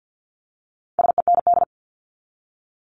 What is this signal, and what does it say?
Morse code, heard as a keyed tone, about 50 wpm, 700 Hz. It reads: SEND
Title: Top 1251-1500 Words 50wpm